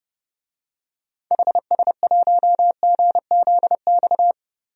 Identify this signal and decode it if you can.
Morse code, heard as a keyed tone, about 30 wpm, 700 Hz. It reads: HS1GZX
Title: DXCC-100 Call Signs - 6 Characters - 30wpm